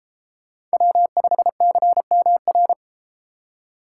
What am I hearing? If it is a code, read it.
Morse code, heard as a keyed tone, about 31 wpm, 700 Hz. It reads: W5CMR